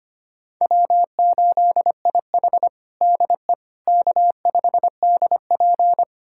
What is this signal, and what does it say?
Morse code, heard as a keyed tone, about 25 wpm, 700 Hz. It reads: W8IH DE K5DP
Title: USA Callsign DE Callsign - 4 Characters - 25wpm